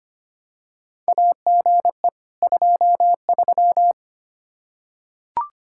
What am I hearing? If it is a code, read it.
Morse code, heard as a keyed tone, about 25 wpm, 700 Hz. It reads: AGE 23
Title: Practice QSO - Exchange 3 - More Info - 25wpm